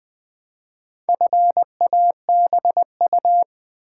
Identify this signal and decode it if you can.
Morse code, heard as a keyed tone, about 20 wpm, 700 Hz. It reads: FABU
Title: Build A Word - 2 Letters At A Time - 8-Letter Words - 20wpm